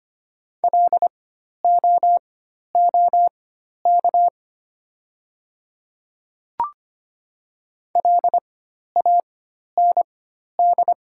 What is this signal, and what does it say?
Morse code, heard as a keyed tone, about 25 wpm, spaced as for 12 wpm, 700 Hz. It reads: LOOK LAND
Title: 4x Character Spacing - Top 200 Words - 25wpm